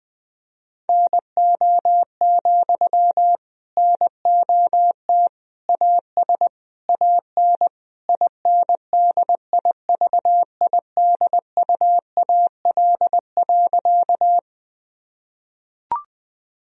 Morse code, heard as a keyed tone, about 19 wpm, 700 Hz. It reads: NO, NOT AS AN INDIVIDUAL.
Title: Sentences from Top 1000 Words - Review Part 4 - 20wpm